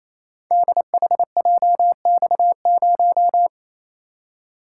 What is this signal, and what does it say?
Morse code, heard as a keyed tone, about 29 wpm, 700 Hz. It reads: DHJX0
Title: Code Groups - 5 Letter:Numbers - 28wpm